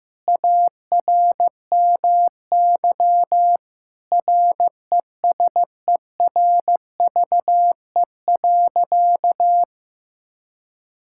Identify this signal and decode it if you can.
Morse code, heard as a keyed tone, about 15 wpm, 700 Hz. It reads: ARMY RESERVE.
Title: Sentences from Top 1000 Words - Review Part 6 - 15wpm